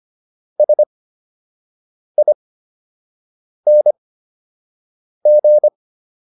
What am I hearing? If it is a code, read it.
Morse code, heard as a keyed tone, about 25 wpm, spaced as for 6 wpm, 600 Hz. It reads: SING